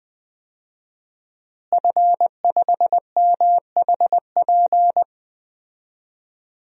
Morse code, heard as a keyed tone, about 20 wpm, 700 Hz. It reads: F5MHP
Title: DXCC-100 Call Signs - 5 Characters - Repeated 2x - 20wpm